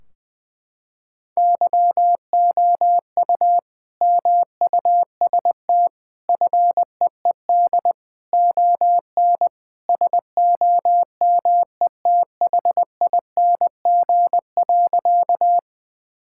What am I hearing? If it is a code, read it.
Morse code, heard as a keyed tone, about 21 wpm, 700 Hz. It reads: YOU MUST FEED ON SOMETHING.
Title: Sentences from Top 1600 Words 20wpm